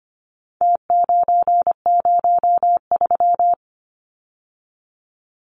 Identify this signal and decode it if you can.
Morse code, heard as a keyed tone, about 25 wpm, 700 Hz. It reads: T903
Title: Numbers - Contest Style - 4-Digit 25wpm